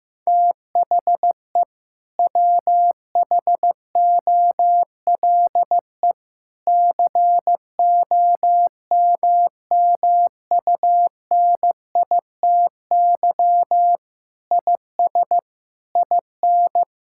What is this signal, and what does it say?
Morse code, heard as a keyed tone, about 15 wpm, 700 Hz. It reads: THE WHOLE COMMUNITY IS IN
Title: Sentences from Top 1600 Words 15wpm